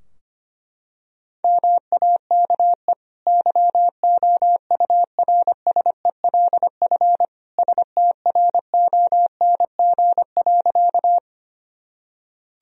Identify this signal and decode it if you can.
Morse code, heard as a keyed tone, about 25 wpm, 700 Hz. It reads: MAKE YOURSELF STRONG.
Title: Sentences from Top 1000 Words - Review Part 3 - 25wpm